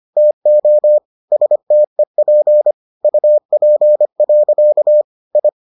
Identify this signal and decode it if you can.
Morse code, heard as a keyed tone, about 25 wpm, 600 Hz. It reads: TO STEP UP. I